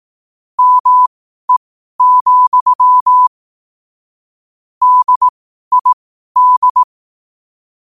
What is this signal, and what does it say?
Morse code, heard as a keyed tone, about 18 wpm, spaced as for 12 wpm, 1000 Hz. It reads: ME, DID
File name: qotdfiles / 12wpm